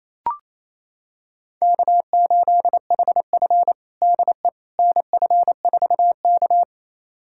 Morse code, heard as a keyed tone, about 28 wpm, 700 Hz. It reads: K8HF DE NF4K
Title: USA Callsign DE Callsign - 4 Characters - 28wpm